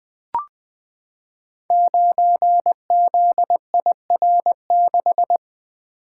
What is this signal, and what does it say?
Morse code, heard as a keyed tone, about 20 wpm, 700 Hz. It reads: 9ZIR6
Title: Code Groups - 5 Letter:Numbers - 20wpm